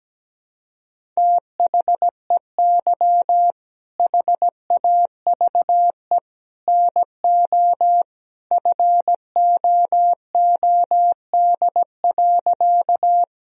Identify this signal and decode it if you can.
Morse code, heard as a keyed tone, about 17 wpm, 700 Hz. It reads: THEY HAVE NO FOOD.
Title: Sentences from Top 1000 Words - Review Part 1 - 17wpm